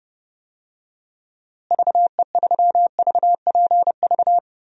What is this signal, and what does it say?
Morse code, heard as a keyed tone, about 30 wpm, 700 Hz. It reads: VE3VPV
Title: DXCC-100 Call Signs - 6 Characters - Repeated 3x - 30wpm